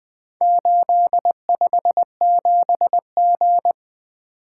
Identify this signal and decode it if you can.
Morse code, heard as a keyed tone, about 20 wpm, 700 Hz. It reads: 857G
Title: Four Letter-Number Combinations - 20wpm